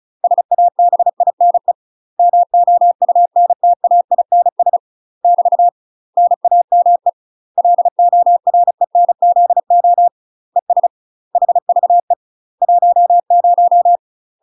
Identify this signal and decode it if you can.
Morse code, heard as a keyed tone, about 34 wpm, 700 Hz. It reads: SABINE MOUNTAINS <BT> NAME LORENZO ES HVE 10
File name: UK MASTER QSOs TRAINING / 35WPM / UK MASTER QSOs 0022